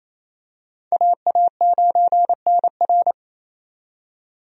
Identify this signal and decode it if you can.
Morse code, heard as a keyed tone, about 27 wpm, 700 Hz. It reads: AA9NR
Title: USA Call Signs - 5 Characters - 28wpm